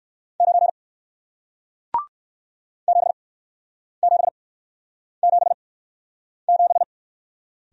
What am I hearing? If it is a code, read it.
Morse code, heard as a keyed tone, about 61 wpm, spaced as for 9 wpm, 700 Hz. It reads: 9 7777